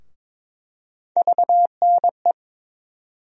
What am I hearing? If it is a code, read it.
Morse code, heard as a keyed tone, about 22 wpm, 700 Hz. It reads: VNE
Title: USA Call Sign Postfixes - 3 Letters 22wpm